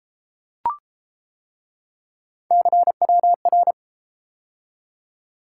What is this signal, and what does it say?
Morse code, heard as a keyed tone, about 31 wpm, 700 Hz. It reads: CWR